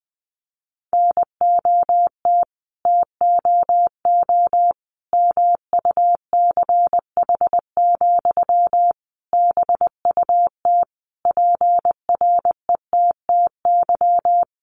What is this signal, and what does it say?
Morse code, heard as a keyed tone, about 20 wpm, 700 Hz. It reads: NOT TOO MUCH, BUT PRETTY
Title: Sentences from Top 1000 Words 20wpm